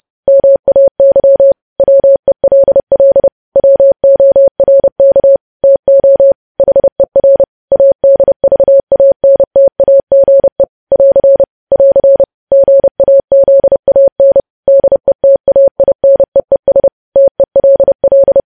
Morse code, heard as a keyed tone, about 30 wpm, 550 Hz. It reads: MAY WELL WORK TO HER ADVANTAGE <AR> <AR> GAZAN DETAINEES TELL
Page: news headlines in morse code at 30 wpm